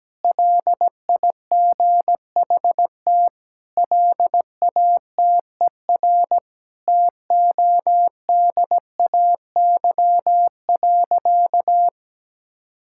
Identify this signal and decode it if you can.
Morse code, heard as a keyed tone, about 17 wpm, 700 Hz. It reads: LIGHT LATER TODAY.
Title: Sentences from Top 1000 Words - Review Part 6 - 17wpm